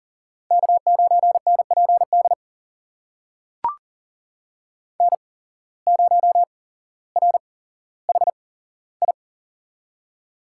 Morse code, heard as a keyed tone, about 40 wpm, spaced as for 17 wpm, 700 Hz. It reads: K9NPD N0RHI
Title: USA Call Signs - 5 Characters - 8x Character Spacing - 40wpm